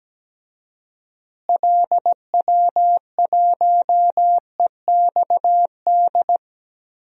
Morse code, heard as a keyed tone, about 17 wpm, 700 Hz. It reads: LW1EXD
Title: DXCC-100 Call Signs - 6 Characters - Repeated 2x - 17wpm